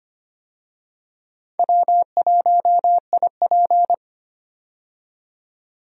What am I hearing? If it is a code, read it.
Morse code, heard as a keyed tone, about 25 wpm, 700 Hz. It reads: W1IP
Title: USA Call Signs - 4 Characters - 25wpm